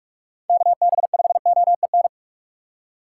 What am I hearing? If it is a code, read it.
Morse code, heard as a keyed tone, about 46 wpm, 700 Hz. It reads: KB5OEN
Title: USA Call Signs - 2x Character Spacing - 45wpm